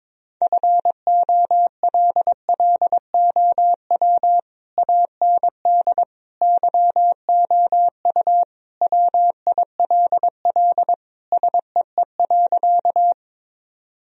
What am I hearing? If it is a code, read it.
Morse code, heard as a keyed tone, about 22 wpm, 700 Hz. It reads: FOLLOW AND YOU WILL SEE.
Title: Sentences from Top 1000 Words - Review Part 1 - 22wpm